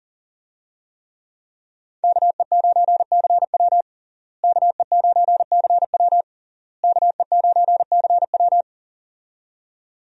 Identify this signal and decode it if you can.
Morse code, heard as a keyed tone, about 40 wpm, 700 Hz. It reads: KE9CW KE9CW KE9CW
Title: USA Call Signs - 5 Characters - Repeated 3x - 40wpm